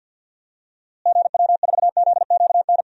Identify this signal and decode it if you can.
Morse code, heard as a keyed tone, about 51 wpm, 700 Hz. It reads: GW4ZQN
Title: DXCC-100 Call Signs - 6 Characters - Repeated 3x - 50wpm